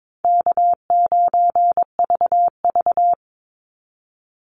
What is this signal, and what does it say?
Morse code, heard as a keyed tone, about 22 wpm, 690 Hz. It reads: K9VV
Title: Call Signs Repeated 2x - 22wpm